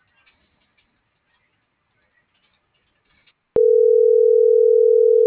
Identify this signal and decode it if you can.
Ringback tone (North America), 440 + 480 Hz.
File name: CENTRAL SOTOVA / out alejandro.rodriguez - .56057